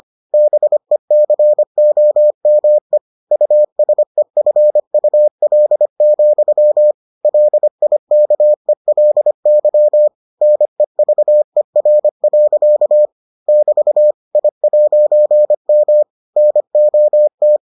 Morse code, heard as a keyed tone, about 25 wpm, 600 Hz. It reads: BECOME USEFUL, LIKELY NEVER. <BT> I'M NOT